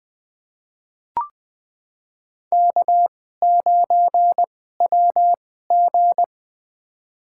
Morse code, heard as a keyed tone, about 21 wpm, spaced as for 15 wpm, 700 Hz. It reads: K9WG
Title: USA Call Signs - 2x Character Spacing - 20wpm